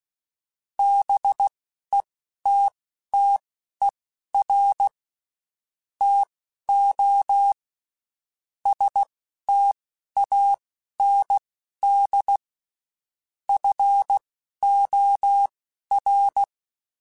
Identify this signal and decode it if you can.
Morse code, heard as a keyed tone, about 16 wpm, spaced as for 11 wpm, 800 Hz. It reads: BETTER TO STAND FOR
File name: Quotes Spaced / Quotes Spaced-mp3s / StandFor sp @15